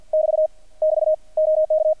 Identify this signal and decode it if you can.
Morse code, heard as a keyed tone, about 50 wpm, 620 Hz. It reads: <BT> <BT> OK
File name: archived / CRSnet